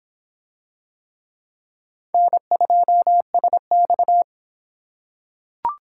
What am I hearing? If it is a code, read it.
Morse code, heard as a keyed tone, about 25 wpm, 700 Hz. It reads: N2SX